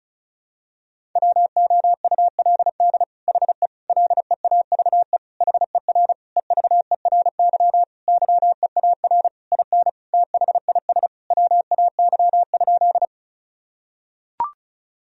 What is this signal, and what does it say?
Morse code, heard as a keyed tone, about 36 wpm, 700 Hz. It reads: WOULD HE LEAVE HER EVERY YEAR IN THIS WAY?
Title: Sentences from Top 1000 Words - Review Part 1 - 35wpm